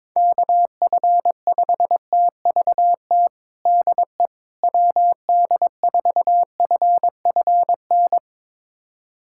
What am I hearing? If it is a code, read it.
Morse code, heard as a keyed tone, about 22 wpm, 700 Hz. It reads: KF5TVT DE WD4FFN
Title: USA Callsign DE Callsign - 6 Characters - 22wpm